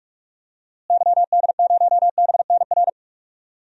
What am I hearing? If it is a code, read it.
Morse code, heard as a keyed tone, about 46 wpm, 700 Hz. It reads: YD0BNR
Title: DXCC-100 Call Signs - 6 Characters - Repeated 3x - 45wpm